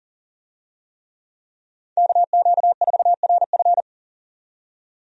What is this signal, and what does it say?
Morse code, heard as a keyed tone, about 40 wpm, 700 Hz. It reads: KQ4RF
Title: USA Call Signs - 5 Characters - 40wpm